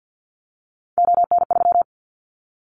Morse code, heard as a keyed tone, about 50 wpm, 700 Hz. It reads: ON3
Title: Callsign Prefixes for Top 100 DXCC - 50wpm